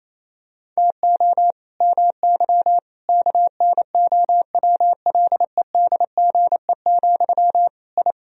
Morse code, heard as a keyed tone, about 28 wpm, 700 Hz. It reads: TO MY KNOWLEDGE, I
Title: Sentences from Top 1000 Words - Review Part 3 - 28wpm